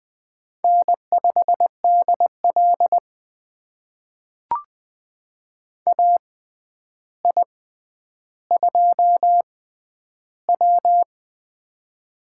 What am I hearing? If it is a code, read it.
Morse code, heard as a keyed tone, about 20 wpm, spaced as for 10 wpm, 700 Hz. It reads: N5DL AI2W
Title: USA Call Signs - 4 Characters - 6x Character Spacing - 20wpm